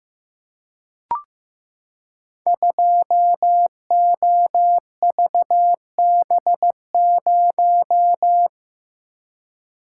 Morse code, heard as a keyed tone, about 15 wpm, 700 Hz. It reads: 2OVB0
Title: Code Groups - 5 Letter:Numbers - 15wpm